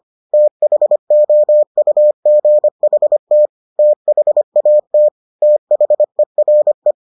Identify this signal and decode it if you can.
Morse code, heard as a keyed tone, about 25 wpm, 600 Hz. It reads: THOUGHT THAT THERE